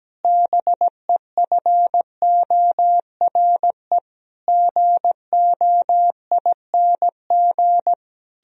Morse code, heard as a keyed tone, about 17 wpm, 700 Hz. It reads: BEFORE GOING
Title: Sentences from Top 1000 Words - Review Part 3 - 17wpm